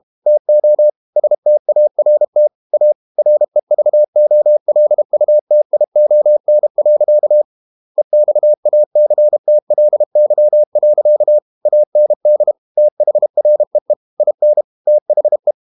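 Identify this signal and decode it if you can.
Morse code, heard as a keyed tone, about 32 wpm, 600 Hz. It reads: TO START A REVOLUTION. EXACTLY. AND THREE IN THE
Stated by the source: Morse code practice at 32 wpm from CW med Gnister, reading Ken Follett's novel Never in English.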